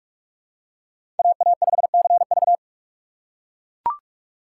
Morse code, heard as a keyed tone, about 41 wpm, 700 Hz. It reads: AA5CV